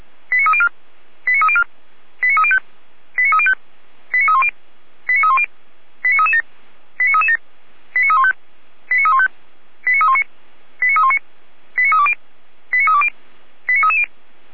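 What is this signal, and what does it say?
Five-tone selective call (ZVEI), heard as quick runs of five tones, tones of about 70 ms. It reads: 89304 89304 89305 89305 89310 89310 89337 89337 89315 89315 89319 89319 89320 89320 89330
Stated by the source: ZVEI five-tone call 89262 received by SDR, logged by Feuerwehr Orlamünde.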